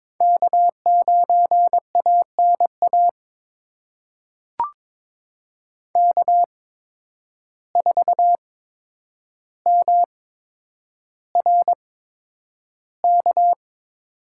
Morse code, heard as a keyed tone, about 22 wpm, spaced as for 9 wpm, 700 Hz. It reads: K9ANA K4MRK
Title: USA Call Signs - 5 Characters - 8x Character Spacing - 22wpm